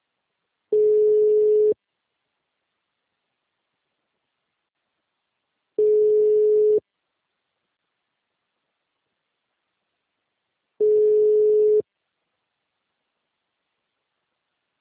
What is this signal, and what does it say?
Ringback tone (Europe, 425 Hz).